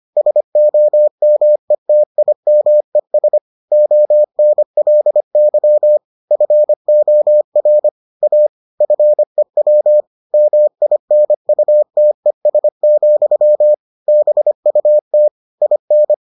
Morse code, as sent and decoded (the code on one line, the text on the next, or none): ... --- -- . - .. -- . ... / --- -. .-.. -.-- / ..-. --- .-. / .- / ..-. .-- / -- .. -. ..- - . ... --..-- / -... ..- - / .. -.
SOMETIMES ONLY FOR A FEW MINUTES, BUT IN